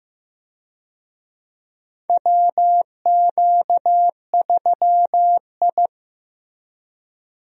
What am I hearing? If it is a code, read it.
Morse code, heard as a keyed tone, about 15 wpm, 700 Hz. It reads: WQ3I
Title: USA Call Signs - 4 Characters - Repeated 2x - 15wpm